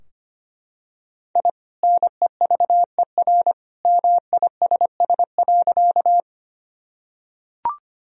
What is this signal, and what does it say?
Morse code, heard as a keyed tone, about 25 wpm, 700 Hz. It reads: I NEVER MISS.
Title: Sentences from Top 1000 Words - Review Part 3 - 25wpm